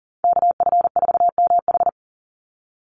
Morse code, heard as a keyed tone, about 40 wpm, 700 Hz. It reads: KF4MH
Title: NA SOTA Chaser Callsigns 40wpm